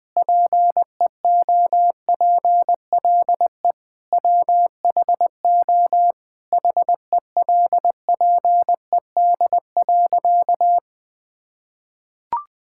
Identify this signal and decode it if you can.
Morse code, heard as a keyed tone, about 21 wpm, 700 Hz. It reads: PEOPLE WHO HELPED.
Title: Sentences with 4-Word Phrases - Spoken Beginning - 20wpm